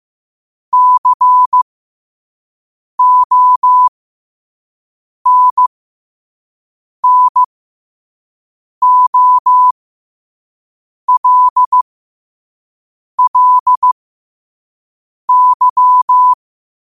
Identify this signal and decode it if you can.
Morse code, heard as a keyed tone, about 15 wpm, spaced as for 5 wpm, 1000 Hz. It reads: CONNOLLY
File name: qotdfiles / 5wpm